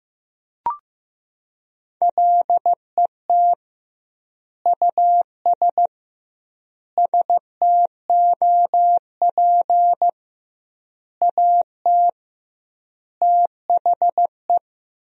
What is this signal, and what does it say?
Morse code, heard as a keyed tone, about 16 wpm, 700 Hz. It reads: LET US STOP AT THE
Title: 2x Word-Spacing - Sentences from Top 400 Words - 15wpm